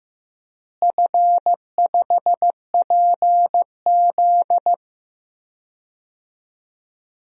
Morse code, heard as a keyed tone, about 15 wpm, 700 Hz. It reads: F5PZ
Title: DXCC-100 Call Signs - 4 Characters - Repeated 3x - 15wpm